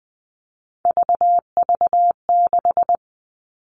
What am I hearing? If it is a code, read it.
Morse code, heard as a keyed tone, about 20 wpm, 700 Hz. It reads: VV6
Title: All Cut Numbers - 3-Digit 20wpm